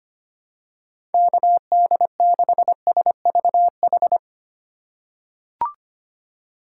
Morse code, heard as a keyed tone, about 25 wpm, 700 Hz. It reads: KD6SVH
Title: USA Call Signs - 6 Characters - 2x Character Spacing - 25wpm